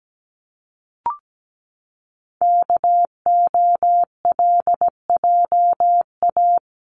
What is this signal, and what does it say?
Morse code, heard as a keyed tone, about 17 wpm, 700 Hz. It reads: KOLJA
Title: CWT Exchange 17wpm